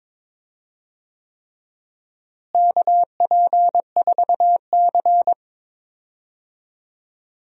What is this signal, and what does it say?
Morse code, heard as a keyed tone, about 22 wpm, 700 Hz. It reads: KP4C
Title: DXCC-100 Call Signs - 4 Characters - Repeated 3x - 22wpm